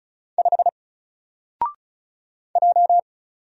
Morse code, heard as a keyed tone, about 36 wpm, 700 Hz. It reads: 5 J
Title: Single Character (Letter:Number:Punctuation:Prosign) 35wpm